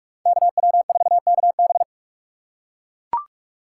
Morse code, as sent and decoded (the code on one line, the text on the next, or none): -.- .-- ....- -.- -...
KW4KB